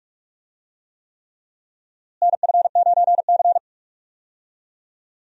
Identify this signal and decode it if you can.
Morse code, heard as a keyed tone, about 46 wpm, 700 Hz. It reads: NF9C
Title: USA Call Signs - 4 Characters - Repeated 3x - 45wpm